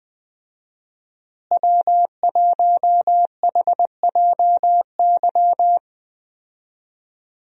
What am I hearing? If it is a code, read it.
Morse code, heard as a keyed tone, about 21 wpm, 700 Hz. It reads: W1HJY
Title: USA Call Signs - 5 Characters - Repeated 2x - 20wpm